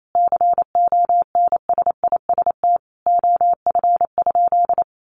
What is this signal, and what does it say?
Morse code, heard as a keyed tone, about 28 wpm, 700 Hz. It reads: CONSIST OF?
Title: Sentences from Top 1400 Words 28wpm